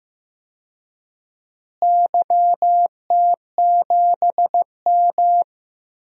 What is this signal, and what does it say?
Morse code, heard as a keyed tone, about 15 wpm, 700 Hz. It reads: YT7M
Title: DXCC-100 Call Signs - 4 Characters - Repeated 3x - 15wpm